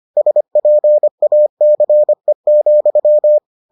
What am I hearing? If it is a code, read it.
Morse code, heard as a keyed tone, about 25 wpm, 600 Hz. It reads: SPACE,